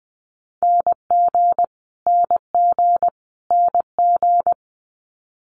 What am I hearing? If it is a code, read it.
Morse code, heard as a keyed tone, about 19 wpm, 700 Hz. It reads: NG NG NG
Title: Multi-letter Phonemes Repeated 3x 20wpm